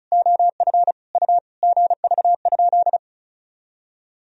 Morse code, heard as a keyed tone, about 34 wpm, 690 Hz. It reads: OF U GV?
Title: Sentences from Top 200 Words and Abbreviations 35wpm